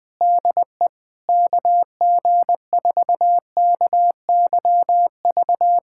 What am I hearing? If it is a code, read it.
Morse code, heard as a keyed tone, about 20 wpm, 700 Hz. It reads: DE KG4KYV
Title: USA Callsign DE Callsign - 6 Characters - 20wpm